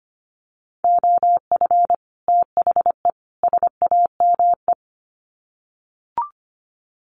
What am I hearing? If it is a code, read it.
Morse code, heard as a keyed tone, about 25 wpm, 700 Hz. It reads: OF THE SAME
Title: Sets of 3 Words 25wpm